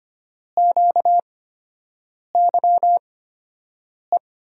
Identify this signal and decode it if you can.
Morse code, heard as a keyed tone, about 25 wpm, spaced as for 7 wpm, 700 Hz. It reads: QYE
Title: USA Call Signs - 6 Characters - 8x Character Spacing - 25wpm